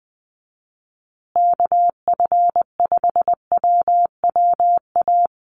Morse code, heard as a keyed tone, about 20 wpm, 700 Hz. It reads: KF5WWA
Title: USA Call Signs Repeated 3x 20wpm